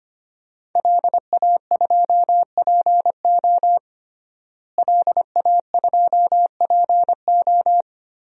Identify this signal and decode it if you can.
Morse code, heard as a keyed tone, about 25 wpm, 700 Hz. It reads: LA2PO LA2PO
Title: DXCC-100 Call Signs - 5 Characters - Repeated 3x - 25wpm